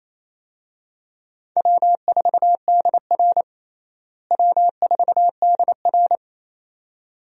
Morse code, heard as a keyed tone, about 27 wpm, 700 Hz. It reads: W4DR W4DR
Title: USA Call Signs - 4 Characters - Repeated 2x - 28wpm